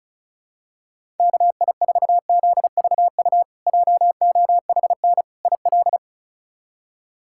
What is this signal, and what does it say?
Morse code, heard as a keyed tone, about 34 wpm, 700 Hz. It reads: KI4ZVU JOHN IL
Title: K1USN SST QSO Practice - 35wpm